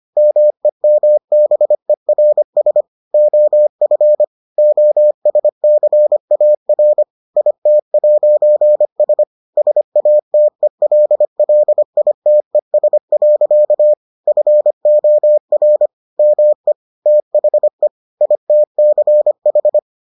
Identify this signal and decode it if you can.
Morse code, heard as a keyed tone, about 25 wpm, 600 Hz. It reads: MEMBERS OF OSCAR IT'S SATELLITES. FOR ME THE ITCH